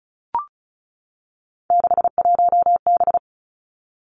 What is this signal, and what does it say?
Morse code, heard as a keyed tone, about 36 wpm, 700 Hz. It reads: 61B
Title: Three Challenging Letter-Number Combinations 35wpm